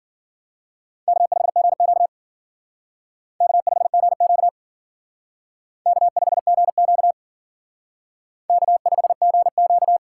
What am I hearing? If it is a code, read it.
Morse code, heard as a keyed tone, about 51 wpm, 700 Hz. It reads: K5GQ K5GQ K5GQ K5GQ